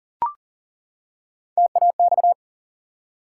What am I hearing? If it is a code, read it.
Morse code, heard as a keyed tone, about 40 wpm, 700 Hz. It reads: TAX
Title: Three Letter Words - 40wpm